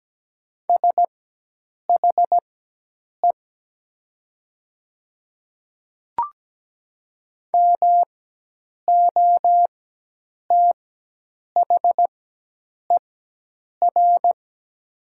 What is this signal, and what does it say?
Morse code, heard as a keyed tone, about 17 wpm, spaced as for 8 wpm, 700 Hz. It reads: SHE MOTHER
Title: 4x Character Spacing - Top 200 Words - 17wpm